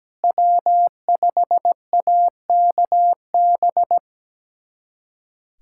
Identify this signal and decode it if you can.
Morse code, heard as a keyed tone, about 17 wpm, 700 Hz. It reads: W5AKB
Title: USA Callsign DE Callsign - 5 Characters - 17wpm